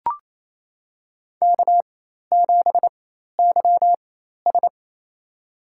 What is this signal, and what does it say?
Morse code, heard as a keyed tone, about 28 wpm, spaced as for 13 wpm, 700 Hz. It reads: K7YS
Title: USA Call Signs - 4 Characters - 4x Character Spacing - 28wpm